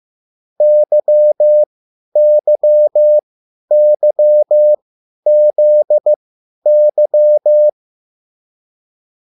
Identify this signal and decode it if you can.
Morse code, heard as a keyed tone, about 15 wpm, spaced as for 10 wpm, 600 Hz. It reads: YYYZY
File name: QSL / CW lesson